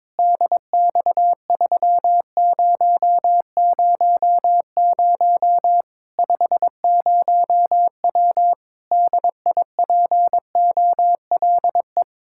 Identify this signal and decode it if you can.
Morse code, heard as a keyed tone, about 22 wpm, 700 Hz. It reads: DX3000 50W DIPOLE
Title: Practice QSO - Exchange 2 - Info - 22wpm